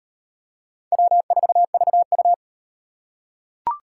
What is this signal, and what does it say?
Morse code, heard as a keyed tone, about 36 wpm, 700 Hz. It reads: W4VU